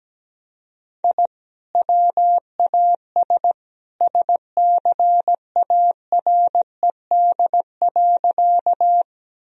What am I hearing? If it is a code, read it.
Morse code, heard as a keyed tone, about 17 wpm, 700 Hz. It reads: I WAS SCARED.
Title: Sentences from Top 1000 Words - Review Part 4 - 17wpm